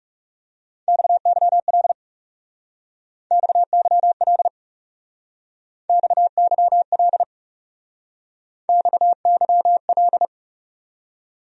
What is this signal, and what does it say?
Morse code, heard as a keyed tone, about 40 wpm, 700 Hz. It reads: XYL XYL XYL XYL